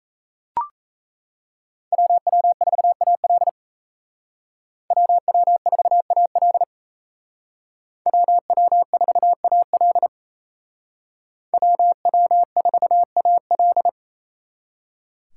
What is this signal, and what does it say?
Morse code, heard as a keyed tone, about 36 wpm, 700 Hz. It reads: WW4AL WW4AL WW4AL WW4AL